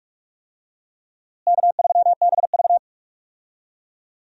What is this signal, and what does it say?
Morse code, heard as a keyed tone, about 41 wpm, 700 Hz. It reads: K3BV
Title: USA Call Signs - 4 Characters - Repeated 3x - 45wpm